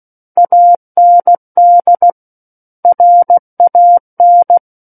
Morse code, heard as a keyed tone, about 16 wpm, 700 Hz. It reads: AND RAN
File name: Book One / 07-Chapter 7 - 16 WPM